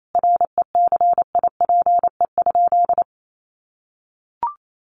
Morse code, heard as a keyed tone, about 28 wpm, 700 Hz. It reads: RECIPE?
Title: Sentences from Top 1200 Words 28wpm